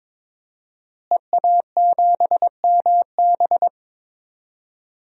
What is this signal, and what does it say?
Morse code, heard as a keyed tone, about 22 wpm, 700 Hz. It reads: EA7MB
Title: DXCC-100 Call Signs - 5 Characters - Repeated 2x - 22wpm